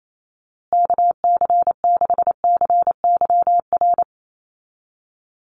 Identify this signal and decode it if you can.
Morse code, heard as a keyed tone, about 28 wpm, 700 Hz. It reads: KC6CYR
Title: USA Call Signs Repeated 2x 28wpm